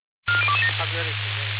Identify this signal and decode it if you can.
Five-tone selective call (ZVEI), heard as quick runs of five tones, tones of about 70 ms, with speech. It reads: 40227